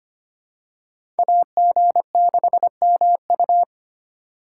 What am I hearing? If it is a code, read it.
Morse code, heard as a keyed tone, about 25 wpm, 700 Hz. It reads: AG6MU
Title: USA Call Signs - 2x Character Spacing - 25wpm